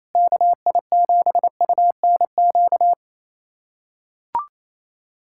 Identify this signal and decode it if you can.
Morse code, heard as a keyed tone, about 28 wpm, 700 Hz. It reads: KI7UNQ
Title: USA Callsign DE Callsign - 6 Characters - 28wpm